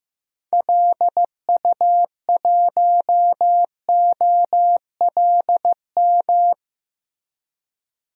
Morse code, heard as a keyed tone, about 15 wpm, 700 Hz. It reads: LU1OLM
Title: DXCC-100 Call Signs - 6 Characters - Repeated 3x - 15wpm